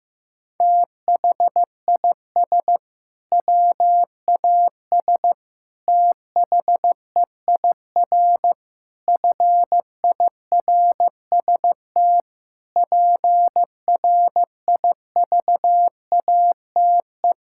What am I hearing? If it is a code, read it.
Morse code, heard as a keyed tone, about 16 wpm, 700 Hz. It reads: THIS WAS THEIR FIRST PRIVATE
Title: Sentences from Top 1600 Words 15wpm